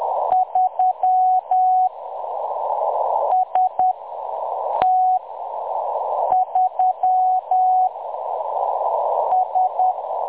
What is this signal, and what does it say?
Morse code, heard as a keyed tone, about 10 wpm, spaced as for 6 wpm, 740 Hz. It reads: 3 ST3 S
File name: M23-ST3- -0858utc-10184khz